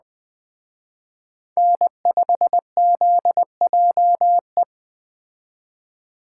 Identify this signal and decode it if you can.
Morse code, heard as a keyed tone, about 20 wpm, 700 Hz. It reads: N5ZJE
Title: Speed Racing - USA 1x3 Call Signs 17wpm